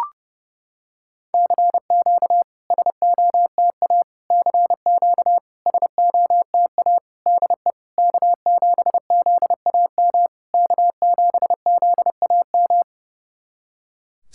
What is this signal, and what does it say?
Morse code, heard as a keyed tone, about 30 wpm, 690 Hz. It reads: CQ SOTA CQ SOTA DE K7ZAM K7ZAM
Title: SOTA QSO Practice 30wpm